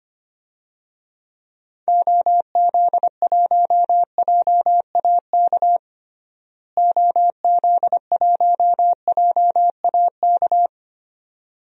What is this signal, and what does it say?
Morse code, heard as a keyed tone, about 25 wpm, 700 Hz. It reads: OZ1JAK OZ1JAK
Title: DXCC-100 Call Signs - 6 Characters - Repeated 2x - 25wpm